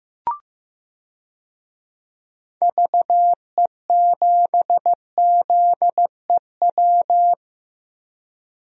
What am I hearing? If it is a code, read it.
Morse code, heard as a keyed tone, about 15 wpm, 700 Hz. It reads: VE7ZEW
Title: DXCC-100 Call Signs - 6 Characters - 15wpm